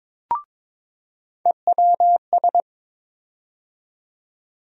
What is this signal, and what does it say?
Morse code, heard as a keyed tone, about 22 wpm, 700 Hz. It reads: EWS
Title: USA Call Sign Postfixes - 3 Letters 22wpm